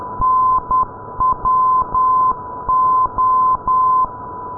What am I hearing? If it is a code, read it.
Morse code, heard as a keyed tone, about 10 wpm, 1030 Hz. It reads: NWO
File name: pol-505 ndb oksywie nwo-494-kiwi